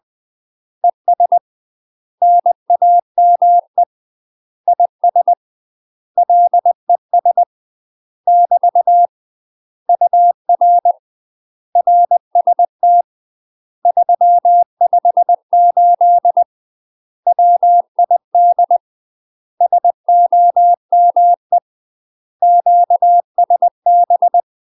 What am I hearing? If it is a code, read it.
Morse code, heard as a keyed tone, about 20 wpm, 700 Hz. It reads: ES NAME IS LES <BT> UR RST 358 WID SOME QSB
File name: UK MASTER QSOs TRAINING / 20WPM / UK MASTER QSOs 0055